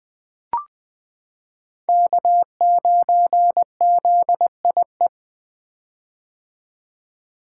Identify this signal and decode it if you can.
Morse code, heard as a keyed tone, about 20 wpm, 700 Hz. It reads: K9ZIE
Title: 2023 POTA Activator Call Signs - 20wpm